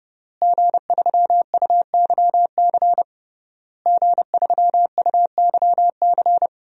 Morse code, heard as a keyed tone, about 30 wpm, 700 Hz. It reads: G3UYC G3UYC
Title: DXCC-100 Call Signs - 5 Characters - Repeated 3x - 30wpm